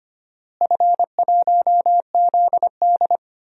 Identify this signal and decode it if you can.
Morse code, heard as a keyed tone, about 25 wpm, 700 Hz. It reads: F1ZD
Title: DXCC-100 Call Signs - 4 Characters - Repeated 2x - 25wpm